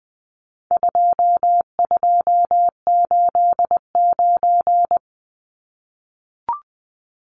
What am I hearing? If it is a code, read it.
Morse code, heard as a keyed tone, about 20 wpm, 700 Hz. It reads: 2289
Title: Numbers - 4-Digit 20wpm